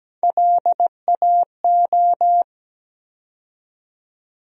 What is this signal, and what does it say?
Morse code, heard as a keyed tone, about 17 wpm, 700 Hz. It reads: LAO
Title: USA Call Sign Postfixes - 3 Letters 17wpm